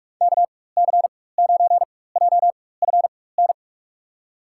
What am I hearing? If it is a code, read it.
Morse code, heard as a keyed tone, about 45 wpm, spaced as for 21 wpm, 700 Hz. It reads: KC9JFN
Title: USA Call Signs - 6 Characters - 4x Character Spacing - 45wpm